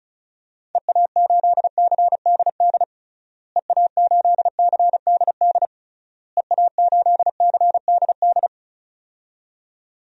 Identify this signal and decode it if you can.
Morse code, heard as a keyed tone, about 36 wpm, 700 Hz. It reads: EA8CDD EA8CDD EA8CDD
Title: DXCC-100 Call Signs - 6 Characters - Repeated 3x - 35wpm